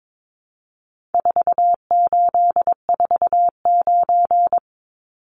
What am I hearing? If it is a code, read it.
Morse code, heard as a keyed tone, about 22 wpm, 700 Hz. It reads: 4849
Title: Numbers - 4-Digit 22wpm